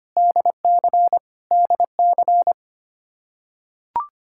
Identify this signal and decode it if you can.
Morse code, heard as a keyed tone, about 25 wpm, 700 Hz. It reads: DC DC
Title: Practice QSO - Exchange 1 - Beginning - 25wpm